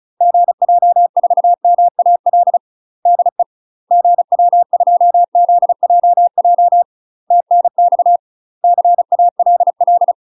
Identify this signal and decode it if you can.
Morse code, heard as a keyed tone, about 36 wpm, 700 Hz. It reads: GJ4MAL DE GW2ZJJ TNX CALL
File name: UK MASTER QSOs TRAINING / 35WPM / UK MASTER QSOs 0019